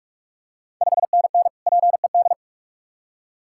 Morse code, heard as a keyed tone, about 46 wpm, 700 Hz. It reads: 5NN PED
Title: Ontario QSO Party Exchange 45wpm